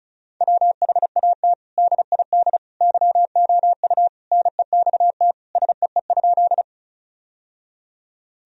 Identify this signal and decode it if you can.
Morse code, heard as a keyed tone, about 36 wpm, 700 Hz. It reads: WHAT DID YOU NEXT SEE?
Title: Sentences from Top 1000 Words - Review Part 1 - 35wpm